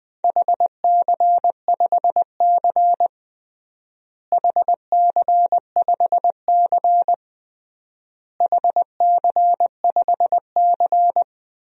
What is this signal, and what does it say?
Morse code, heard as a keyed tone, about 21 wpm, 700 Hz. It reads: HC5C HC5C HC5C
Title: DXCC-100 Call Signs - 4 Characters - Repeated 3x - 20wpm